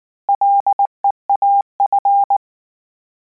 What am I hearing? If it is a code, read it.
Morse code, heard as a keyed tone, about 19 wpm, 800 Hz. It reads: LEAF